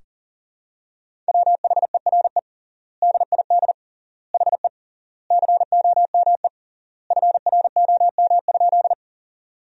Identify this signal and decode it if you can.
Morse code, heard as a keyed tone, about 40 wpm, 700 Hz. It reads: WHERE DID HE COME FROM?
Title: 3x Word-Spacing - Sentences from Top 200 Words - 40wpm